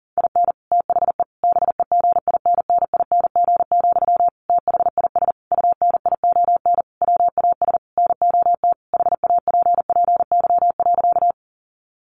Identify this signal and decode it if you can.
Morse code, heard as a keyed tone, about 40 wpm, 700 Hz. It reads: IN THE BEGINNING, THIS UNION WAS NOT HAPPY.
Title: Sentences from Top 1400 Words 40wpm